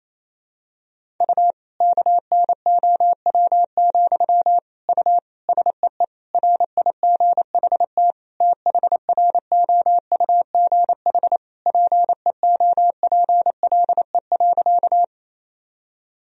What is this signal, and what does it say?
Morse code, heard as a keyed tone, about 28 wpm, 700 Hz. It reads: U KNOW, U SEE RIGHT THROUGH PEOPLE.
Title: Sentences from Top 200 Words and Abbreviations 28wpm